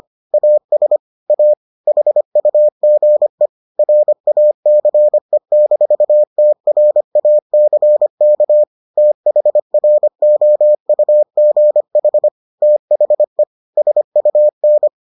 Morse code, heard as a keyed tone, about 25 wpm, 600 Hz. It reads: AS A HUGE RACE-TRACK THROUGH THE SUN